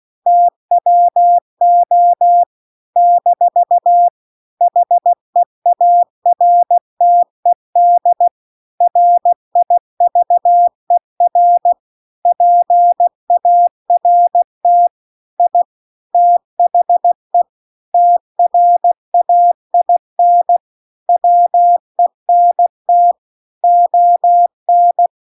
Morse code, heard as a keyed tone, about 16 wpm, 700 Hz. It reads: TWO - HEARTED RIVER PART I THE TRAIN WENT ON
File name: CW-The-Big-Twoheaded-River-1-16WPM